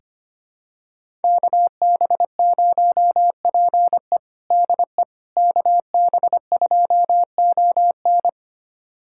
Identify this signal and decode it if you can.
Morse code, heard as a keyed tone, about 25 wpm, 700 Hz. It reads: KB0PE DE KB2ON
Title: USA Callsign DE Callsign - 5 Characters - 25wpm